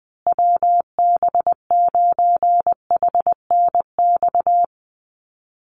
Morse code, heard as a keyed tone, about 20 wpm, 700 Hz. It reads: WB9HNX
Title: USA Call Signs Repeated 3x 20wpm